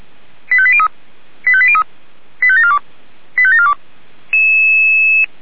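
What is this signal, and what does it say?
Five-tone selective call (ZVEI), heard as quick runs of five tones, tones of about 70 ms. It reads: 85803 85803 85742 85742
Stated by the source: ZVEI five-tone selective call received on SDR.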